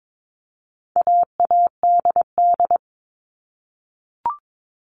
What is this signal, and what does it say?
Morse code, heard as a keyed tone, about 22 wpm, 700 Hz. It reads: AADD
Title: All Cut Numbers - 4-Digit 22wpm